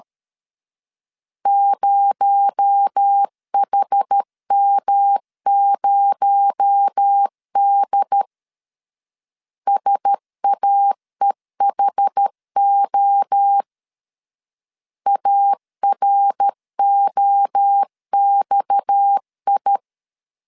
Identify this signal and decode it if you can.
Morse code, heard as a keyed tone, about 13 wpm, 800 Hz. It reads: 0HM0D SAEHO AROXI
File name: LEZIONI / 6 PKX0 / 6 PKX0 12wpm 2231